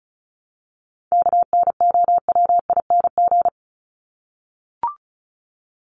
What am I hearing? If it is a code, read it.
Morse code, heard as a keyed tone, about 34 wpm, 700 Hz. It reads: KNOWING
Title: ING Suffix - Top 100 Words with Base Words 35wpm